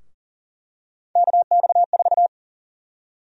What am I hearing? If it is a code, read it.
Morse code, heard as a keyed tone, about 40 wpm, 700 Hz. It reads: KX4
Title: USA Call Sign Prefixes - 40wpm